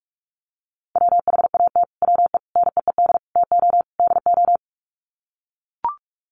Morse code, heard as a keyed tone, about 44 wpm, 700 Hz. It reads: WHAT WE NEED TO DO
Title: Sets of 5 Words Encore 45wpm